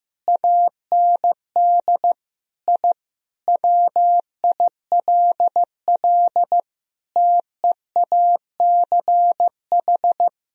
Morse code, heard as a keyed tone, about 15 wpm, 700 Hz. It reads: AND I WILL TEACH
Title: Sentences from Top 1000 Words - Review Part 2 - 15wpm